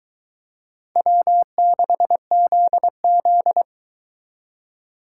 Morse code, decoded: W6ZZ

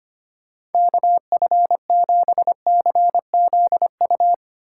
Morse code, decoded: KF7CZU

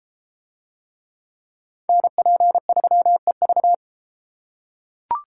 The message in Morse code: NP3EV

